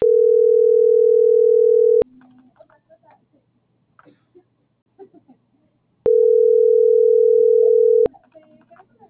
Ringback tone (North America), 440 + 480 Hz.